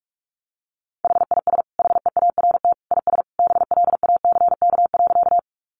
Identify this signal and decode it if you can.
Morse code, heard as a keyed tone, about 44 wpm, 700 Hz. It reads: HIS HEART IS BLACK.
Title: Sentences from Top 1200 Words 45wpm